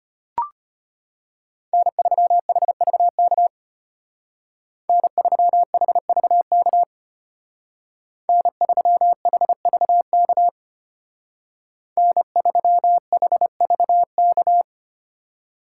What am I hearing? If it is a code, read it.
Morse code, heard as a keyed tone, about 32 wpm, 700 Hz. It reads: N3HVK N3HVK N3HVK N3HVK